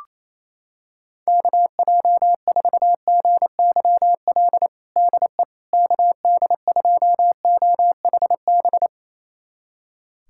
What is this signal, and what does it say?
Morse code, heard as a keyed tone, about 28 wpm, 700 Hz. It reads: KJ4GYL DE KD2OHB
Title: USA Callsign DE Callsign - 6 Characters - 28wpm